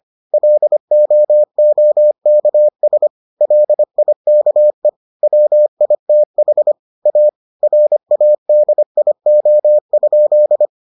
Morse code, heard as a keyed tone, about 25 wpm, 600 Hz. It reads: LOOKS LIKE WITH A RADIO?